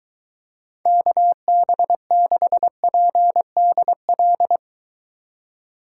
Morse code, heard as a keyed tone, about 23 wpm, 700 Hz. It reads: KB6PDL